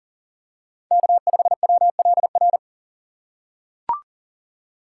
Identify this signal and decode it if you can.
Morse code, heard as a keyed tone, about 40 wpm, 700 Hz. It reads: K5WLR
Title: USA Call Signs - 5 Characters - 3x Character Spacing - 40wpm